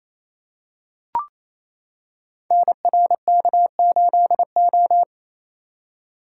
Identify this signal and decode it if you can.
Morse code, heard as a keyed tone, about 28 wpm, 700 Hz. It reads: NRK8O
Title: Code Groups - 5 Letter:Numbers - 28wpm